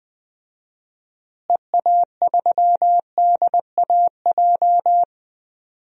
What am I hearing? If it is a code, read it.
Morse code, heard as a keyed tone, about 20 wpm, 700 Hz. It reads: EA3DAJ
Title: DXCC-100 Call Signs - 6 Characters - Repeated 3x - 20wpm